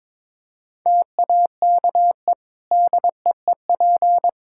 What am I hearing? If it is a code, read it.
Morse code, heard as a keyed tone, about 22 wpm, 700 Hz. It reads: TAKE DEEP
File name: practice / 22wpm / Easy Sentences with Hints - 22wpm